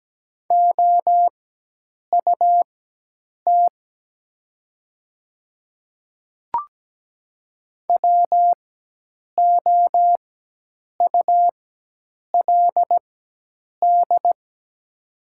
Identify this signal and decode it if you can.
Morse code, heard as a keyed tone, about 17 wpm, spaced as for 8 wpm, 700 Hz. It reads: OUT WOULD
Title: 4x Character Spacing - Top 100 Words - 17wpm